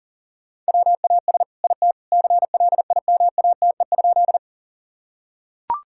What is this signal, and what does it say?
Morse code, heard as a keyed tone, about 37 wpm, 700 Hz. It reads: WAS IT CLIMATE?
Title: Sentences from Top 1600 Words 40wpm